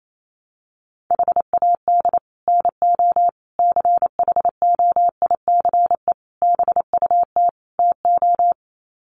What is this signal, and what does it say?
Morse code, heard as a keyed tone, about 28 wpm, 700 Hz. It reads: HAD NO CHOICE BUT TO
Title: Sets of 5 Words 28wpm